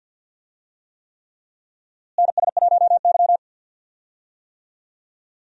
Morse code, heard as a keyed tone, about 51 wpm, 700 Hz. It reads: NS1Y
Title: USA Call Signs - 4 Characters - Repeated 3x - 50wpm